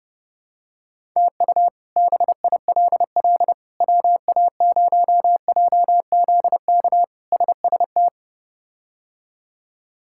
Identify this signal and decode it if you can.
Morse code, heard as a keyed tone, about 30 wpm, 700 Hz. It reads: TU BILL WA0JZK SST
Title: K1USN SST QSO Practice - 30wpm